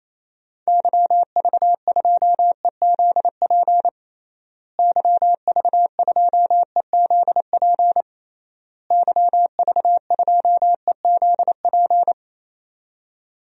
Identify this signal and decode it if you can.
Morse code, heard as a keyed tone, about 28 wpm, 700 Hz. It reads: YV2EZP YV2EZP YV2EZP
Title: DXCC-100 Call Signs - 6 Characters - Repeated 3x - 28wpm